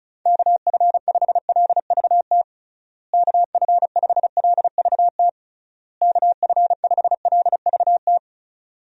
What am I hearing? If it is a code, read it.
Morse code, heard as a keyed tone, about 36 wpm, 700 Hz. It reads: KF5LVT KF5LVT KF5LVT
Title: USA Call Signs - 6 Characters - Repeated 3x - 35wpm